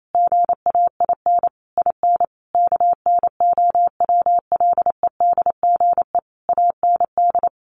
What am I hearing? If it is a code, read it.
Morse code, heard as a keyed tone, about 28 wpm, 700 Hz. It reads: GAIN IN KNOWLEDGE AND
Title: Sentences from Top 1500 Words 28wpm